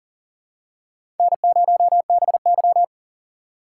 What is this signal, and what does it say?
Morse code, heard as a keyed tone, about 40 wpm, 700 Hz. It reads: N0BY
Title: USA Call Signs - 4 Characters - 2x Character Spacing - 40wpm